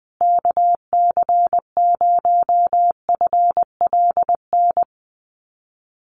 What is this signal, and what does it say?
Morse code, heard as a keyed tone, about 20 wpm, 700 Hz. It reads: KC0FLN
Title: USA Call Signs Repeated 3x 20wpm